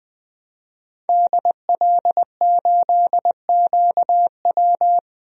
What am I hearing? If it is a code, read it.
Morse code, heard as a keyed tone, about 20 wpm, 700 Hz. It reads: DL8QW
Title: DXCC-100 Call Signs - 5 Characters - 20wpm